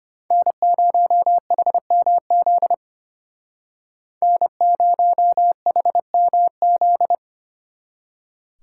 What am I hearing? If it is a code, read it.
Morse code, heard as a keyed tone, about 32 wpm, 700 Hz. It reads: N0HMZ N0HMZ